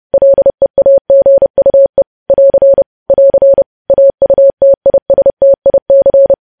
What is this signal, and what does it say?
Morse code, heard as a keyed tone, about 30 wpm, 550 Hz. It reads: LEAGUE <AR> <AR> AUTISTIC
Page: news headlines in morse code at 30 wpm